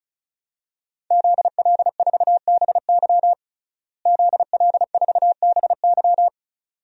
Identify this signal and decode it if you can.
Morse code, heard as a keyed tone, about 36 wpm, 700 Hz. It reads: ZL4BY ZL4BY
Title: DXCC-100 Call Signs - 5 Characters - Repeated 2x - 35wpm